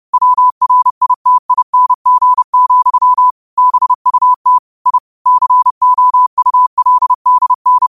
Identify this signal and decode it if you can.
Morse code, heard as a keyed tone, about 30 wpm, 1000 Hz. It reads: WRITING, BUT I COULDN
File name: qotdfiles / 30wpm